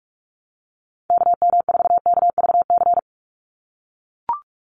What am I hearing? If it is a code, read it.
Morse code, heard as a keyed tone, about 44 wpm, 700 Hz. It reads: KM4KVC
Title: USA Call Signs 45wpm